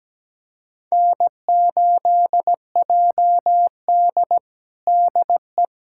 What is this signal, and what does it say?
Morse code, heard as a keyed tone, about 17 wpm, 700 Hz. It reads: N8JD DE
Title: USA Callsign DE Callsign - 4 Characters - 17wpm